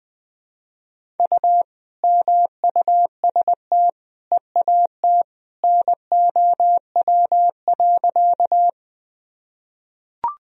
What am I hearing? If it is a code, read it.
Morse code, heard as a keyed tone, about 20 wpm, 700 Hz. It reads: U MUST EAT NOW.
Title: Sentences from Top 300 Words and Abbreviations 20wpm